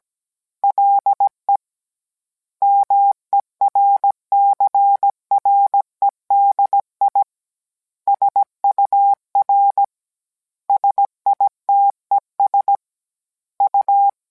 Morse code, heard as a keyed tone, about 17 wpm, 800 Hz. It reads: LE MERCREDI SUR SITES U